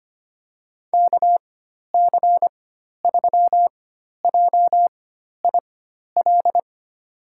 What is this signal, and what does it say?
Morse code, heard as a keyed tone, about 25 wpm, spaced as for 12 wpm, 700 Hz. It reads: KC3JIL
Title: USA Call Signs - 4x Character Spacing - 25wpm